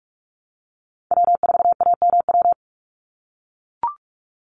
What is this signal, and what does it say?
Morse code, heard as a keyed tone, about 44 wpm, 700 Hz. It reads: W4AMW
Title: NA SOTA Chaser Callsigns 45wpm